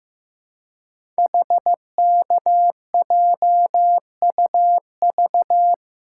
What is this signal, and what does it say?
Morse code, heard as a keyed tone, about 15 wpm, 700 Hz. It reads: HKJUV
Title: Code Groups - 5 Letters - 15wpm